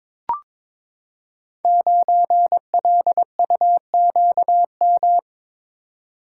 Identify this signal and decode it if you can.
Morse code, heard as a keyed tone, about 22 wpm, 690 Hz. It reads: 9LUQM
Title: Code Groups - 5 Letter:Numbers - 22wpm